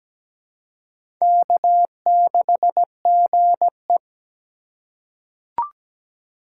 Morse code, heard as a keyed tone, about 17 wpm, 700 Hz. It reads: K6GE
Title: USA Call Signs - 4 Characters - 3x Character Spacing - 17wpm